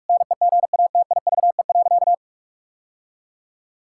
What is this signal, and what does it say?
Morse code, heard as a keyed tone, about 46 wpm, 700 Hz. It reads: NEGATIVE.
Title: Sentences from Top 1000 Words - Review Part 6 - 45wpm